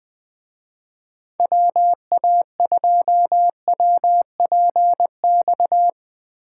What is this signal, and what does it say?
Morse code, heard as a keyed tone, about 20 wpm, 700 Hz. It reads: WA2WPX